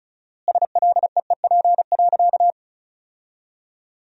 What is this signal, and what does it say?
Morse code, heard as a keyed tone, about 36 wpm, 700 Hz. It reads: SLEEP.
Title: 4x Word-Spacing - Sentences from Top 500 Words - 35wpm